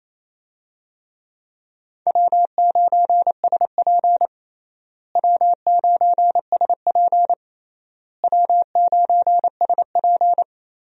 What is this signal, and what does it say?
Morse code, heard as a keyed tone, about 28 wpm, 700 Hz. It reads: W9SP W9SP W9SP
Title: USA Call Signs - 4 Characters - Repeated 3x - 28wpm